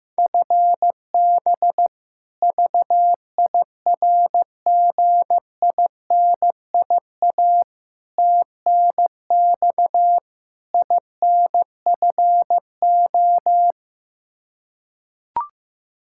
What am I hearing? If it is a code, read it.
Morse code, heard as a keyed tone, about 15 wpm, 700 Hz. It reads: FB VIRGINIA TNX INFO
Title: K1USN SST QSO Practice 15wpm